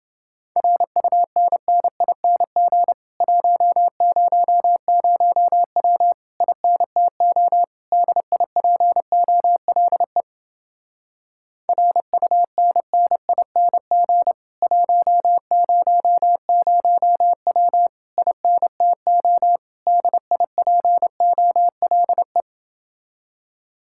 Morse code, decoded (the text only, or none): RUNNING 100W INTO DIPOLE RUNNING 100W INTO DIPOLE